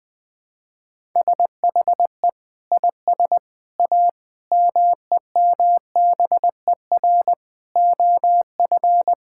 Morse code, heard as a keyed tone, about 20 wpm, 700 Hz. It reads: SHE IS A MEMBER OF
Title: Sentences with 4-Word Phrases - Spoken Beginning - 20wpm